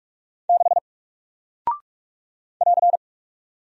Morse code, heard as a keyed tone, about 46 wpm, 700 Hz. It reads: 6 <AR>
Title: Single Character (Letter:Number:Punctuation:Prosign) Alt 45wpm